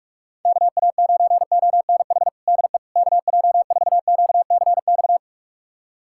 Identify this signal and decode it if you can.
Morse code, heard as a keyed tone, about 46 wpm, 700 Hz. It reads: KA9ONH DE KJ4QCX
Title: USA Callsign DE Callsign - 6 Characters - 45wpm